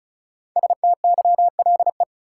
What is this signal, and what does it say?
Morse code, heard as a keyed tone, about 34 wpm, 700 Hz. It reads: STYLE
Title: Sentences from Top 800 Words - New Word Drill - 35wpm